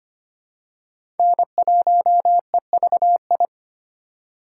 Morse code, heard as a keyed tone, about 25 wpm, 700 Hz. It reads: N1EVI
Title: USA Call Signs - 5 Characters - 2x Character Spacing - 25wpm